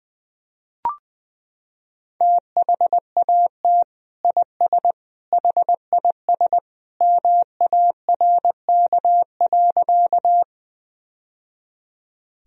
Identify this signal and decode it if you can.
Morse code, heard as a keyed tone, about 20 wpm, 700 Hz. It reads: THAT IS HIS MARK.
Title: Sentences from Top 1000 Words - Review Part 1 - 20wpm